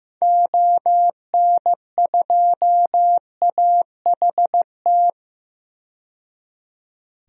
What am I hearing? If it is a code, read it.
Morse code, heard as a keyed tone, about 16 wpm, 700 Hz. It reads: ON2AHT
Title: DXCC-100 Call Signs - 6 Characters - 15wpm